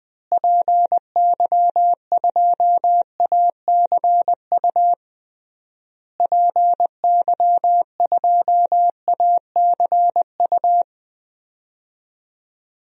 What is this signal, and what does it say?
Morse code, heard as a keyed tone, about 20 wpm, 700 Hz. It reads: PY2ACU PY2ACU
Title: DXCC-100 Call Signs - 6 Characters - Repeated 3x - 20wpm